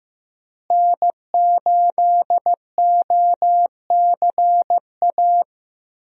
Morse code, heard as a keyed tone, about 15 wpm, 700 Hz. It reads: N8OCA
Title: USA Call Signs - 5 Characters - Repeated 3x - 15wpm